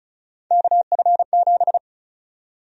Morse code, heard as a keyed tone, about 34 wpm, 700 Hz. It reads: KF7
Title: USA Call Sign Prefixes - 35wpm